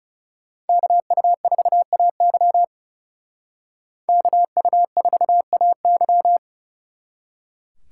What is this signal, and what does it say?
Morse code, heard as a keyed tone, about 38 wpm, 700 Hz. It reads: KU4AY KU4AY